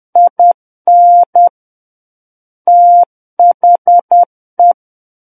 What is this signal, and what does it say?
Morse code, heard as a keyed tone, about 10 wpm, 700 Hz. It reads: IN THE
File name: Book One / 01-Chapter 1 - 10 WPM